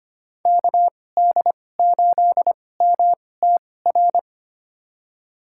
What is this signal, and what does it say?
Morse code, heard as a keyed tone, about 25 wpm, spaced as for 18 wpm, 700 Hz. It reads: KD8MTR
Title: USA Call Signs - 6 Characters - 2x Character Spacing - 25wpm